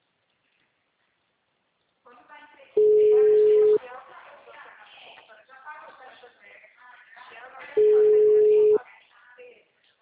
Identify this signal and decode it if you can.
Ringback tone (Europe, 425 Hz), with speech.
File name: out- -2015- - - .452